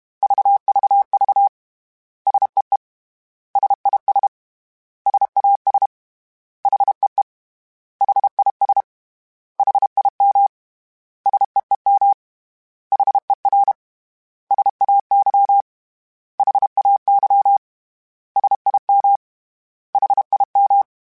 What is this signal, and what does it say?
Morse code, heard as a keyed tone, about 32 wpm, 800 Hz. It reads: VVV SEE SIS SAS HEE HIS HIM SEEM HER SAY HAY SIM HIM
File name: S vs H / S vs H mp3s / S vs H @30